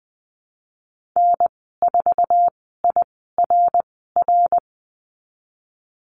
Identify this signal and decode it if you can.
Morse code, heard as a keyed tone, about 20 wpm, spaced as for 15 wpm, 700 Hz. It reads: N4IRR
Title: 055 Lesson 10g - Letter r - Call Signs 20wpm